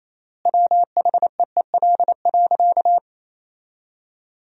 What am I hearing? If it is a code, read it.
Morse code, heard as a keyed tone, about 28 wpm, 700 Hz. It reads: WHEEL.
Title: 4x Word-Spacing - Sentences from Top 500 Words - 28wpm